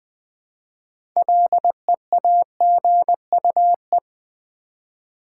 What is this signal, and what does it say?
Morse code, heard as a keyed tone, about 21 wpm, 700 Hz. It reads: LEAGUE
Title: Six Letter Words - 20wpm